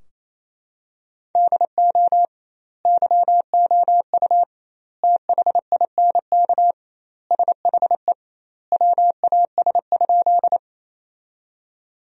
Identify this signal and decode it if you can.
Morse code, heard as a keyed tone, about 28 wpm, 700 Hz. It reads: DO YOU THINK SHE WAS?
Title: 2x Word-Spacing - Sentences from Top 200 Words - 28wpm